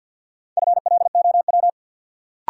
Morse code, heard as a keyed tone, about 53 wpm, 700 Hz. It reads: FLOW
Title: Sentences from Top 1200 Words - New Word Drill Alt - 50wpm